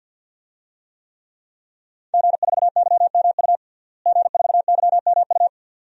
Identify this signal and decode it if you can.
Morse code, heard as a keyed tone, about 53 wpm, 700 Hz. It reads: G4YMU G4YMU
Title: DXCC-100 Call Signs - 5 Characters - Repeated 2x - 50wpm